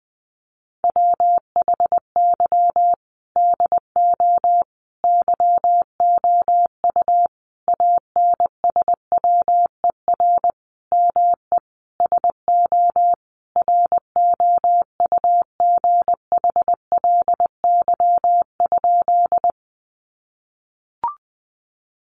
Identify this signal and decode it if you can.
Morse code, heard as a keyed tone, about 19 wpm, 700 Hz. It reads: WHY DO YOU ANSWER ME SO ROUGHLY?
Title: Sentences from Top 1500 Words 20wpm